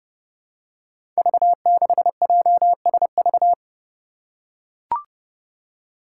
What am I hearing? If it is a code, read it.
Morse code, heard as a keyed tone, about 30 wpm, 700 Hz. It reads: V6JSV
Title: Five Challenging Letter-Number Combinations 30wpm